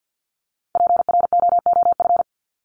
Morse code, heard as a keyed tone, about 50 wpm, 700 Hz. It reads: PROOF
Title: Top 1501-1750 Words 50wpm